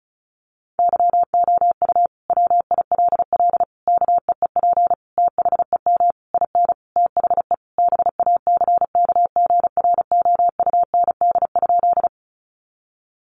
Morse code, heard as a keyed tone, about 34 wpm, 700 Hz. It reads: YOU WILL KEEP THEM IN THE BACKGROUND?
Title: Sentences from Top 1300 Words 35wpm